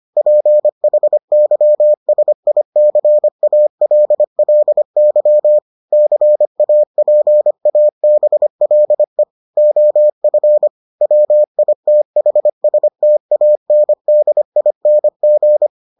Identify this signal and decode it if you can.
Morse code, heard as a keyed tone, about 25 wpm, 600 Hz. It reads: PHYSICALLY CAPABLE OF WITHSTANDING